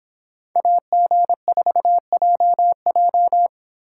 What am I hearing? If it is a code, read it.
Morse code, heard as a keyed tone, about 25 wpm, 700 Hz. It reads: AG4JJ